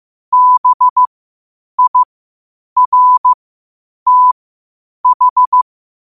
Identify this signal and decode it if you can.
Morse code, heard as a keyed tone, about 15 wpm, spaced as for 8 wpm, 1000 Hz. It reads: BIRTH
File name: qotdfiles / 8wpm